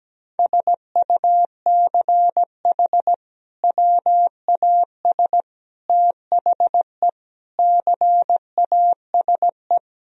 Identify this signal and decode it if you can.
Morse code, heard as a keyed tone, about 17 wpm, 700 Hz. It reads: SUCH WAS THE CASE
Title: Sentences from Top 1000 Words - Review Part 1 - 17wpm